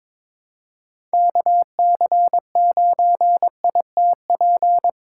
Morse code, heard as a keyed tone, about 23 wpm, 700 Hz. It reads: KC9ITP
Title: USA Call Signs - 6 Characters - 4x Character Spacing - 22wpm